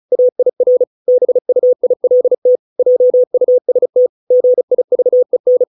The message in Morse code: AIR BUILT JUST GIVEN

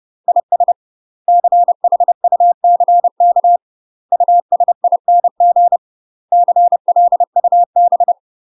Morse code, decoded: IS CHUCK USING CLUB

